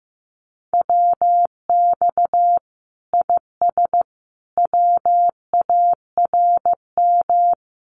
Morse code, heard as a keyed tone, about 15 wpm, 700 Hz. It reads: WX IS WARM
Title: Common QSO Phrases 15wpm